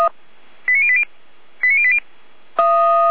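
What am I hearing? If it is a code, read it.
Five-tone selective call (ZVEI), heard as quick runs of five tones, tones of about 80 ms. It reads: 89080 89080
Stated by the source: ZVEI five-tone selective call received on an SDR, fire brigade alerting.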